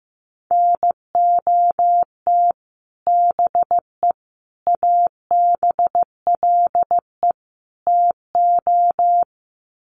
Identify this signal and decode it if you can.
Morse code, heard as a keyed tone, about 15 wpm, 700 Hz. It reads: NOT BE ABLE TO
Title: Sets of 4 Words 15wpm